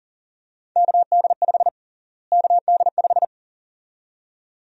Morse code, heard as a keyed tone, about 40 wpm, 700 Hz. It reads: KD5 KD5
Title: USA Call Sign Prefixes - Repeated 2x - 40wpm